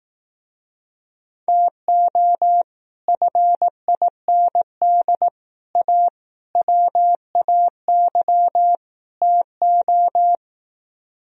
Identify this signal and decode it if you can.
Morse code, heard as a keyed tone, about 18 wpm, 700 Hz. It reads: TO FIND A WAY TO
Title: Speed-Racing - Sets of 5 Words 15wpm